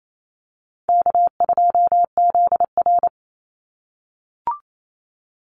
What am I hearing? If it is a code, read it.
Morse code, heard as a keyed tone, about 28 wpm, 700 Hz. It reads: K2ZR
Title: Call Signs - 28wpm